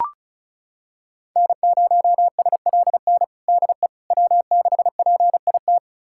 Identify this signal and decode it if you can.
Morse code, heard as a keyed tone, about 36 wpm, 690 Hz. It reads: N0SLN DE W6PIT
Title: USA Callsign DE Callsign - 5 Characters - 35wpm